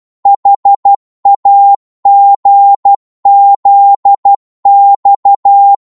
Morse code, heard as a keyed tone, about 12 wpm, 800 Hz. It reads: HAGZX
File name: Morse / ltr-12wpm